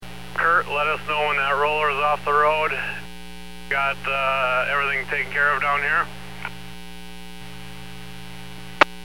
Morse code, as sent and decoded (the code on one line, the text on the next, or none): none